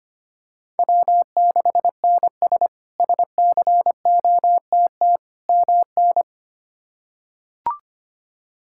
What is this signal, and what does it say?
Morse code, heard as a keyed tone, about 25 wpm, 700 Hz. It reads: W6NS SCOTT MN
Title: K1USN SST QSO Practice - 25wpm